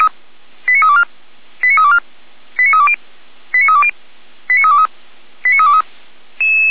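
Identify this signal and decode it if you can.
Five-tone selective call (ZVEI), heard as quick runs of five tones, tones of about 70 ms. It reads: 89324 89324 89320 89320 89323 89323
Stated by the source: ZVEI five-tone call received by SDR, recorded by feuerwehr-orlamuende fire brigade server.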